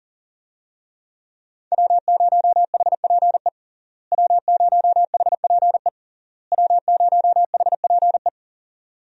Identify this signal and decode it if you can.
Morse code, heard as a keyed tone, about 37 wpm, 700 Hz. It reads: W0HPE W0HPE W0HPE
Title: USA Call Signs - 5 Characters - Repeated 3x - 40wpm